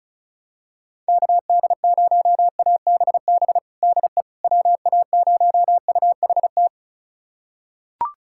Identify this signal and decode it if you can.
Morse code, heard as a keyed tone, about 36 wpm, 700 Hz. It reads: KD0ABB DE WA0UHT
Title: USA Callsign DE Callsign - 6 Characters - 35wpm